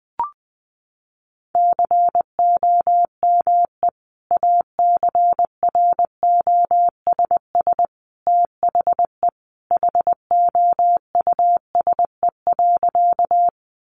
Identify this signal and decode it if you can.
Morse code, heard as a keyed tone, about 19 wpm, 700 Hz. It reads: COME ACROSS THE HOUSE.
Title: Sentences from Top 700 Words 20wpm